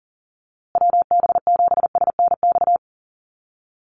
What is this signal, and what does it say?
Morse code, heard as a keyed tone, about 40 wpm, 700 Hz. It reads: WB7SNX
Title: USA Call Signs Repeated 3x 40wpm